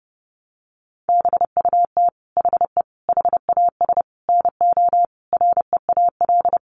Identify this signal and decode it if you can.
Morse code, heard as a keyed tone, about 30 wpm, 700 Hz. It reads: BUT HE HAS NO REAL
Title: Sentences from Top 1300 Words 30wpm